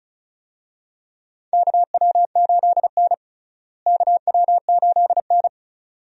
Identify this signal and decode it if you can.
Morse code, heard as a keyed tone, about 36 wpm, 700 Hz. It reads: KW8N KW8N
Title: USA Call Signs - 4 Characters - Repeated 2x - 35wpm